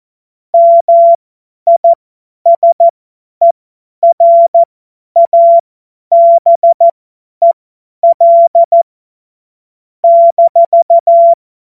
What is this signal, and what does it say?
Morse code, heard as a keyed tone, about 14 wpm, spaced as for 10 wpm, 680 Hz. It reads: MISERABEL -
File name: Morsekurs / Witze / Witz 21. März 2023 50 F70